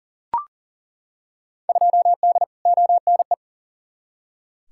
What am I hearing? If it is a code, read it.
Morse code, heard as a keyed tone, about 40 wpm, 700 Hz. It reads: 2D ONE
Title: Field Day Exchanges - 40wpm